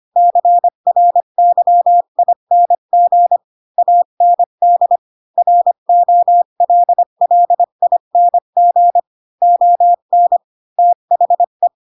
Morse code, heard as a keyed tone, about 25 wpm, 700 Hz. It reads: CRYING AND ROLLING ON THE
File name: Pinocchio / Pinocchio ch7 20 25 ik1hge